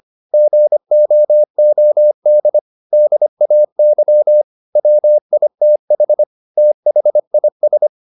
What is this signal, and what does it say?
Morse code, heard as a keyed tone, about 25 wpm, 600 Hz. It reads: GOOD DAY WITH THIS
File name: the old man and the sea part 1 25wpm